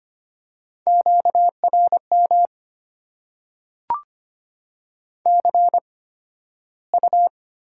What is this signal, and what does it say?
Morse code, heard as a keyed tone, about 25 wpm, spaced as for 13 wpm, 700 Hz. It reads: QRM CU
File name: Top 90 QSO Elements - 8x Character Spacing - 25wpm